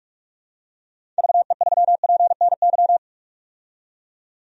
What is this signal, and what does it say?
Morse code, heard as a keyed tone, about 41 wpm, 700 Hz. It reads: VE3PNY
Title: DXCC-100 Call Signs - 6 Characters - 45wpm